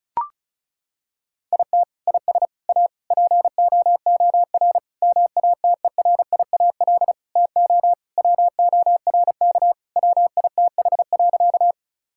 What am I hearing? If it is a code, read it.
Morse code, heard as a keyed tone, about 36 wpm, 700 Hz. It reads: IT IS A POOR MATERIAL TO WORK WITH.
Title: Sentences from Top 1000 Words - Review Part 5 - 35wpm